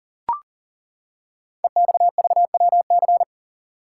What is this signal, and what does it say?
Morse code, heard as a keyed tone, about 40 wpm, 700 Hz. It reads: EXVWC
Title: Code Groups - 5 Letters - 40wpm